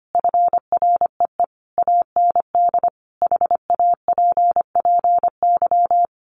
Morse code, heard as a keyed tone, about 25 wpm, 700 Hz. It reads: FREE AND HAPPY
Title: Sentences from Top 1000 Words 25wpm